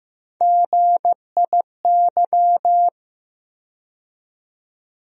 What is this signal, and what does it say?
Morse code, heard as a keyed tone, about 15 wpm, 700 Hz. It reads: GIY
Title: USA Call Sign Postfixes - 3 Letters 15wpm